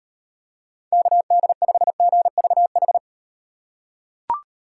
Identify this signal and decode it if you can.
Morse code, heard as a keyed tone, about 38 wpm, 700 Hz. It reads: KD5GVH